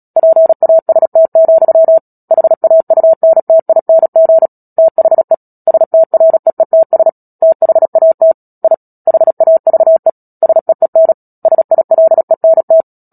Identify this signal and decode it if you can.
Morse code, heard as a keyed tone, about 37 wpm, 650 Hz. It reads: PAST, HAUNTING THE STREETS THAT I HAVE SEEN SILENT
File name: Book Two / 10-Chapter 10 - Epilogue - 36 WPM